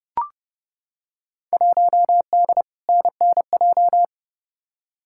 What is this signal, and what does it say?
Morse code, heard as a keyed tone, about 30 wpm, 700 Hz. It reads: 1D NNJ
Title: Field Day Exchanges Encore - 30wpm